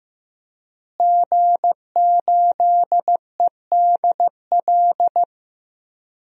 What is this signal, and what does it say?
Morse code, heard as a keyed tone, about 16 wpm, 700 Hz. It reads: G8EDL
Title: DXCC-100 Call Signs - 5 Characters - Repeated 2x - 15wpm